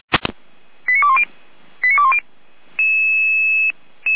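Five-tone selective call (ZVEI), heard as quick runs of five tones, tones of about 70 ms. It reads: 89210 89210
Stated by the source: ZVEI five-tone call 89084 received by SDR.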